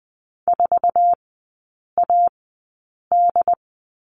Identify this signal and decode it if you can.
Morse code, heard as a keyed tone, about 20 wpm, spaced as for 8 wpm, 700 Hz. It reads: 4AD
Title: 127 Lesson 21e - Number 6 - Three Characters 20wpm